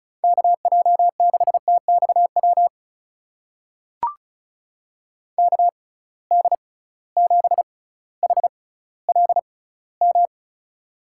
Morse code, heard as a keyed tone, about 36 wpm, spaced as for 18 wpm, 700 Hz. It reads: KJ6TXW KD7HLM
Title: USA Call Signs - 6 Characters - 6x Character Spacing - 35wpm